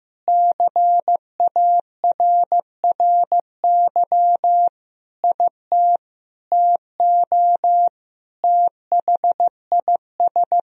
Morse code, heard as a keyed tone, about 15 wpm, 700 Hz. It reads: CARRY IT TO THIS
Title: Sentences from Top 1000 Words - Review Part 1 - 15wpm